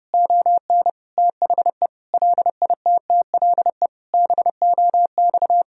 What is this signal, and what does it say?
Morse code, heard as a keyed tone, about 30 wpm, 700 Hz. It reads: ON THE LITTLE BOX
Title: Sentences from Top 1000 Words - Review Part 2 - 30wpm